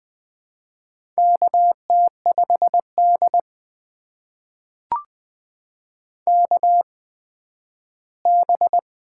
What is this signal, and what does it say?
Morse code, heard as a keyed tone, about 20 wpm, spaced as for 12 wpm, 700 Hz. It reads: KT5D KB
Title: USA Call Signs - 4 Characters - 8x Character Spacing - 20wpm